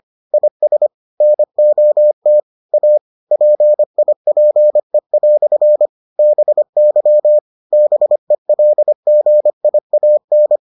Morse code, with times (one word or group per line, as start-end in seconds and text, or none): 0.3-0.9 IS
1.2-2.4 NOT
2.7-3.0 A
3.3-5.9 PIPE"
6.2-7.4 BY
7.7-10.6 BELGIAN